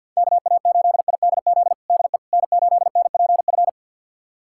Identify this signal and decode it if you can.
Morse code, heard as a keyed tone, about 49 wpm, 700 Hz. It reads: KA8IDZ DE N8NPF
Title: USA Callsign DE Callsign 50wpm